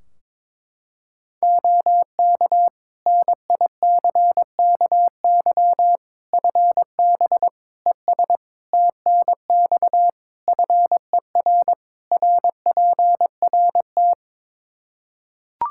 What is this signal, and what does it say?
Morse code, heard as a keyed tone, about 21 wpm, 700 Hz. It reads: OK NICKY FB ES TNX FER RPRT
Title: Practice QSO - Exchange 2 - Info - 22wpm